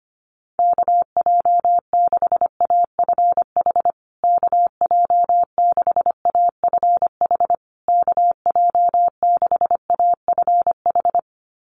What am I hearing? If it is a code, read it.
Morse code, heard as a keyed tone, about 25 wpm, 700 Hz. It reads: KJ6AFH KJ6AFH KJ6AFH
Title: USA Call Signs Repeated 3x 25wpm